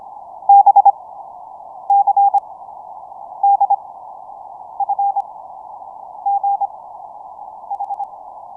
Morse code, heard as a keyed tone, about 25 wpm, 800 Hz. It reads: B C D F G H
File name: signalNoise / MorseABC BPF100Hz